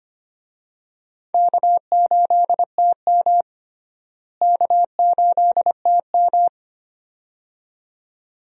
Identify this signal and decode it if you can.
Morse code, heard as a keyed tone, about 25 wpm, 700 Hz. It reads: K8TM K8TM
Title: USA Call Signs - 4 Characters - Repeated 2x - 25wpm